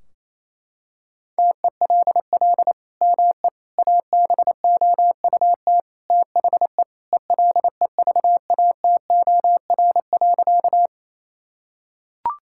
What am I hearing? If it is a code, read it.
Morse code, heard as a keyed tone, about 28 wpm, 700 Hz. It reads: TELL ME ABOUT THE ELEVATOR.
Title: Sentences from Top 1600 Words 28wpm